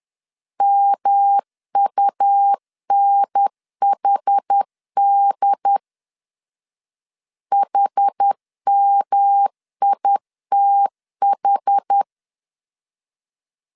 Morse code, heard as a keyed tone, about 10 wpm, 800 Hz. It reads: MUNHD HMITH
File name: CW / DET / 1 3 1500C S B 10wpm 3335